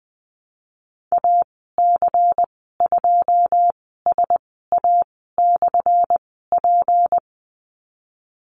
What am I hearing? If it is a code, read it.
Morse code, heard as a keyed tone, about 19 wpm, spaced as for 14 wpm, 700 Hz. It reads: AC2SA/P
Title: 179 Lesson 29g - Slash - Call Signs 20wpm